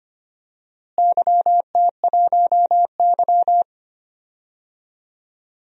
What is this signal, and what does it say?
Morse code, heard as a keyed tone, about 25 wpm, 700 Hz. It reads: YT1Y
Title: DXCC-100 Call Signs - 4 Characters - Repeated 2x - 25wpm